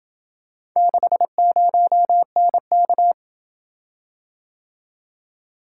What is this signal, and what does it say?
Morse code, heard as a keyed tone, about 26 wpm, 700 Hz. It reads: 60NK